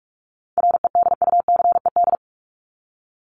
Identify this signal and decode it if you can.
Morse code, heard as a keyed tone, about 44 wpm, 700 Hz. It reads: REDUCED
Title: ED Suffix - Top 100 Words with Base Words 45wpm